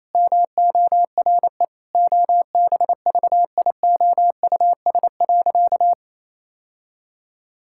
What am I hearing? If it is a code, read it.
Morse code, heard as a keyed tone, about 28 wpm, 700 Hz. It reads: MORE OBVIOUS.
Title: Sentences from Top 1600 Words 28wpm